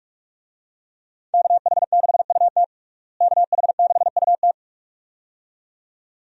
Morse code, heard as a keyed tone, about 46 wpm, 700 Hz. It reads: KH6UT KH6UT
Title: USA Call Signs - 5 Characters - Repeated 2x - 45wpm